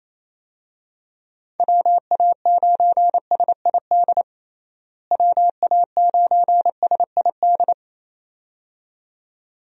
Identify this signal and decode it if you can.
Morse code, heard as a keyed tone, about 28 wpm, 700 Hz. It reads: WA9SID WA9SID
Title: USA Call Signs - 6 Characters - Repeated 2x - 28wpm